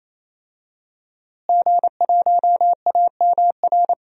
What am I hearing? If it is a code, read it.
Morse code, heard as a keyed tone, about 28 wpm, 690 Hz. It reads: G1AMR
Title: DXCC-100 Call Signs - 5 Characters - Repeated 3x - 28wpm